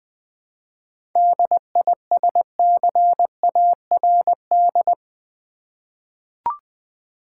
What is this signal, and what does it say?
Morse code, heard as a keyed tone, about 20 wpm, 700 Hz. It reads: DISCARD
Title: 7 Letter Words - First 3 Letters Given - 20wpm